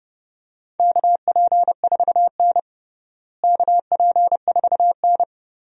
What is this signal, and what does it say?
Morse code, heard as a keyed tone, about 30 wpm, 700 Hz. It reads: KP4N KP4N
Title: DXCC-100 Call Signs - 4 Characters - Repeated 3x - 30wpm